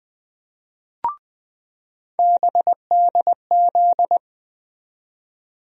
Morse code, heard as a keyed tone, about 20 wpm, 700 Hz. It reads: BDZ
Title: USA Call Sign Postfixes - 3 Letters 20wpm